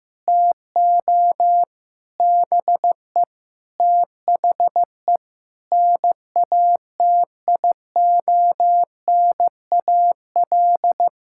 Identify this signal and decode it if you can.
Morse code, heard as a keyed tone, about 15 wpm, 700 Hz. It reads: TO BE THE NATIONAL
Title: Sentences from Top 1000 Words - Review Part 6 - 15wpm